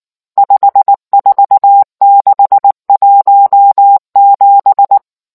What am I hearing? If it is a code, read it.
Morse code, heard as a keyed tone, about 19 wpm, 800 Hz. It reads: 54617